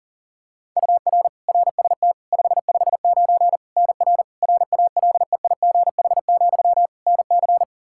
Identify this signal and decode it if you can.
Morse code, heard as a keyed tone, about 40 wpm, 700 Hz. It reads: UR RST 559 NR RALEIGH, NC
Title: Practice QSO - Exchange 1 - Beginning - 40wpm